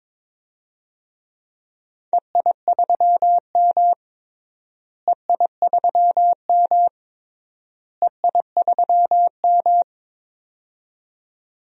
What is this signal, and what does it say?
Morse code, heard as a keyed tone, about 21 wpm, 700 Hz. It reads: EI3M EI3M EI3M
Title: DXCC-100 Call Signs - 4 Characters - Repeated 3x - 22wpm